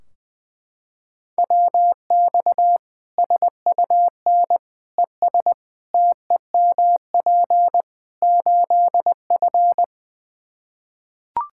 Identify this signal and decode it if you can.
Morse code, heard as a keyed tone, about 20 wpm, 700 Hz. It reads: WX SUN ES TEMP 8F
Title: Practice QSO - Exchange 2 - Info - 20wpm